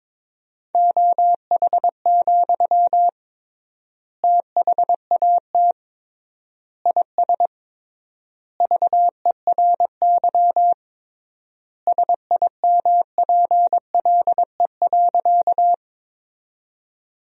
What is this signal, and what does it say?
Morse code, heard as a keyed tone, about 22 wpm, 700 Hz. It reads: OH, THAT IS VERY SIMPLE.
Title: 3x Word-Spacing - Sentences from Top 500 Words - 22wpm